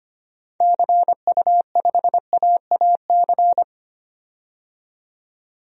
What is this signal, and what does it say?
Morse code, heard as a keyed tone, about 25 wpm, 700 Hz. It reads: CU5AAC
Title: DXCC-100 Call Signs - 6 Characters - 25wpm